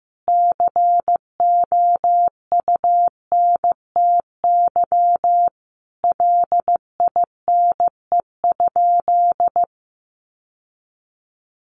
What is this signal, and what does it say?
Morse code, heard as a keyed tone, about 14 wpm, 700 Hz. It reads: COUNTY LINE?
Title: Sentences from Top 1200 Words 15wpm